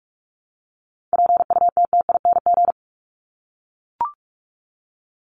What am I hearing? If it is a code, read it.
Morse code, heard as a keyed tone, about 44 wpm, 700 Hz. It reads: PUTTING
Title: ING Suffix - Top 100 Words with Base Words 45wpm